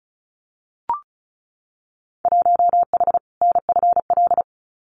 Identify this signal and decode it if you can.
Morse code, heard as a keyed tone, about 35 wpm, 690 Hz. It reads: 1H NFL
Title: Winter Field Day Exchanges 35wpm